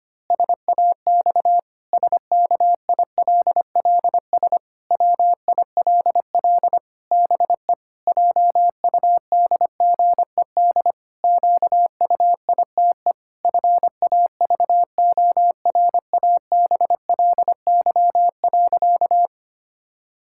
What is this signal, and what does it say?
Morse code, heard as a keyed tone, about 25 wpm, 700 Hz. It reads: SAX SKILLS WILL BE JUDGED QUITE FAVORABLY.
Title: Pangram Challenge v2 - 25wpm